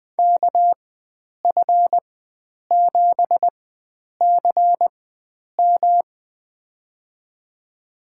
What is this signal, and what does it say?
Morse code, heard as a keyed tone, about 20 wpm, spaced as for 9 wpm, 700 Hz. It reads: KF7CM
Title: USA Call Signs - 5 Characters - 4x Character Spacing - 20wpm